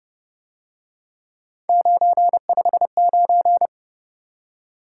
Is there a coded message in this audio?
Yes: Morse code from 1.7 to 3.7 s, heard as a keyed tone, about 29 wpm, 700 Hz. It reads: 959